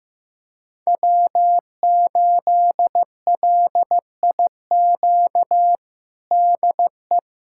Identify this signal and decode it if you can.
Morse code, heard as a keyed tone, about 16 wpm, 700 Hz. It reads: W8LIQ DE
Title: USA Callsign DE Callsign 15wpm